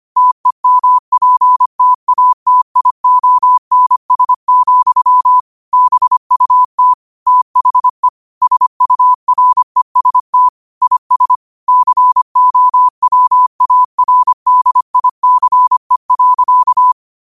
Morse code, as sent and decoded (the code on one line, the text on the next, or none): - . -- .--. - .- - .. --- -. ... --..-- / -... ..- - / - .... / ... ..- .-. ... - / .. ... / -.-. --- .-- .- .-. -.. .. -.-. .-.-.-
TEMPTATIONS, BUT THE SUREST IS COWARDICE.